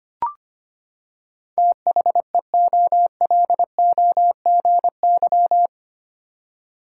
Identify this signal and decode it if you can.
Morse code, heard as a keyed tone, about 25 wpm, 700 Hz. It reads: THEOLOGY
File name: Eight Letter Words - 25wpm